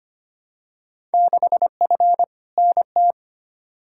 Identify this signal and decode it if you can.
Morse code, heard as a keyed tone, about 25 wpm, 700 Hz. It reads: 6F NT
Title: Field Day Exchanges Encore - 25wpm